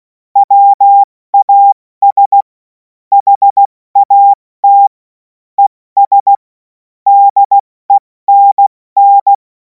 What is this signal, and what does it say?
Morse code, heard as a keyed tone, about 16 wpm, 800 Hz. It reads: WAS HAT ES DENN
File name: Morsekurs / Witze / Witz 28. April 2026 70 F80